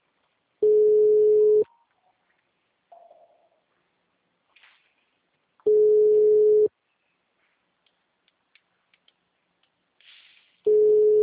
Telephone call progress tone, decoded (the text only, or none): ringback tone